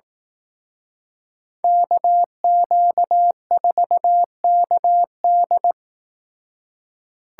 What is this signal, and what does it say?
Morse code, heard as a keyed tone, about 18 wpm, 700 Hz. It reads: KQ4KD
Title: Speed Racing - USA 2x2 Call Signs 15wpm